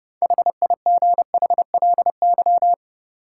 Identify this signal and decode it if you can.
Morse code, heard as a keyed tone, about 30 wpm, 700 Hz. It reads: HIGHLY
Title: Sentences from Top 1100 Words - New Word Drill Alt - 30wpm